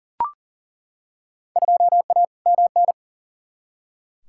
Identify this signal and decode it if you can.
Morse code, heard as a keyed tone, about 39 wpm, 700 Hz. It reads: 2A MN
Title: Field Day Exchanges - 40wpm